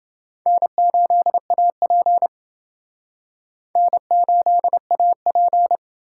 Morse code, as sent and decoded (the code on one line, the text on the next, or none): -. ---.. .- .--. / -. ---.. .- .--.
N8AP N8AP